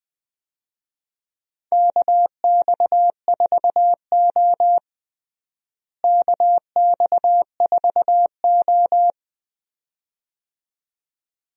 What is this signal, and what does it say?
Morse code, heard as a keyed tone, about 21 wpm, 700 Hz. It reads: KX4O KX4O
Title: USA Call Signs - 4 Characters - Repeated 2x - 20wpm